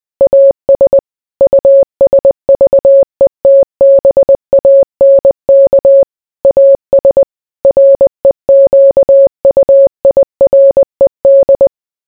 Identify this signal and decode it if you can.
Morse code, heard as a keyed tone, about 20 wpm, 550 Hz. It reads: AS USVETBANK AS REQUIRED